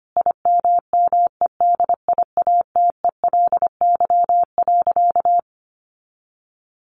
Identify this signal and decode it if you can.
Morse code, heard as a keyed tone, about 25 wpm, 700 Hz. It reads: IMMEDIATELY.
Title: Sentences from Top 1100 Words 25wpm